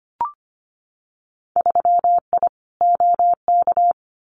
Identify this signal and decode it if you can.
Morse code, heard as a keyed tone, about 25 wpm, 700 Hz. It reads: 3I OK
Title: Winter Field Day Exchanges Encore 25wpm